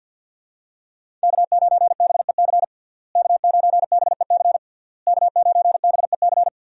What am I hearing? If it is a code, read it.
Morse code, heard as a keyed tone, about 50 wpm, 700 Hz. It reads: K9BEC K9BEC K9BEC
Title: USA Call Signs - 5 Characters - Repeated 3x - 50wpm